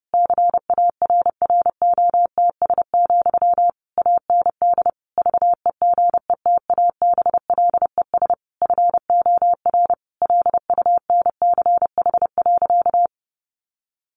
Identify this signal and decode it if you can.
Morse code, heard as a keyed tone, about 30 wpm, 700 Hz. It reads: CARROTS, AND VEGETABLES FOR LUNCH.
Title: Easy Phrases 30wpm